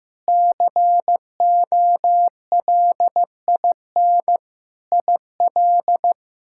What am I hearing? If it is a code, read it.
Morse code, heard as a keyed tone, about 15 wpm, 700 Hz. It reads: COLIN IL
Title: K1USN SST QSO Practice - 15wpm